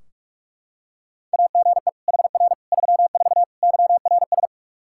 Morse code, heard as a keyed tone, about 42 wpm, 700 Hz. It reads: AGE HR 34 YRS
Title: Practice QSO - Exchange 3 - More Info - 45wpm